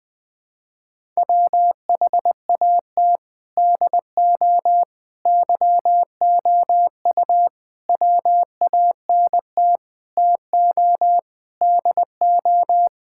Morse code, heard as a keyed tone, about 20 wpm, 700 Hz. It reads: WHAT DO YOU WANT TO DO
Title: Sentences with 4-Word Phrases - 20wpm